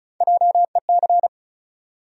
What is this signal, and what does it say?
Morse code, heard as a keyed tone, about 34 wpm, 700 Hz. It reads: JEC
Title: USA Call Sign Postfixes - 3 Letters 35wpm